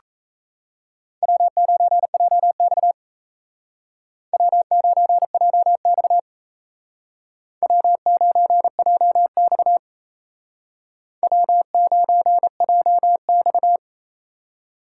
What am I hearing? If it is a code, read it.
Morse code, heard as a keyed tone, about 35 wpm, 700 Hz. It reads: W9JX W9JX W9JX W9JX
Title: Speed Racing - USA 1x2 Call Signs 28wpm